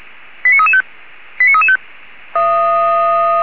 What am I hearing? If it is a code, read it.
Five-tone selective call (ZVEI), heard as quick runs of five tones, tones of about 70 ms. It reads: 89335 89335
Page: SDR/ZVEI